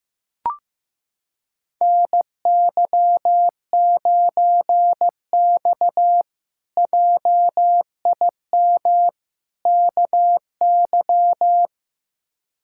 Morse code, heard as a keyed tone, about 16 wpm, 700 Hz. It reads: NY9X JIM KY
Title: K1USN SST QSO Practice - 15wpm